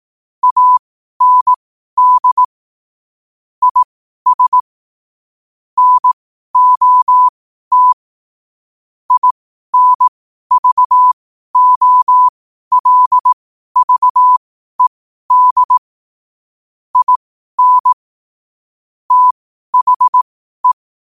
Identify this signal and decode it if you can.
Morse code, heard as a keyed tone, about 18 wpm, spaced as for 12 wpm, 1000 Hz. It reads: AND IS NOT INVOLVED IN THE
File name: qotdfiles / 12wpm